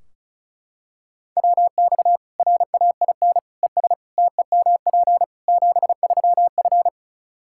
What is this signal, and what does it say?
Morse code, heard as a keyed tone, about 34 wpm, 700 Hz. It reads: WX RAIN ES TEMP 73F
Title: Practice QSO - Exchange 2 - Info - 35wpm